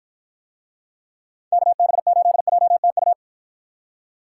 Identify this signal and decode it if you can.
Morse code, heard as a keyed tone, about 53 wpm, 700 Hz. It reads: KB8JTU